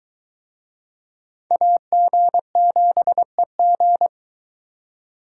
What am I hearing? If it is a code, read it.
Morse code, heard as a keyed tone, about 20 wpm, 700 Hz. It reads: AG7EG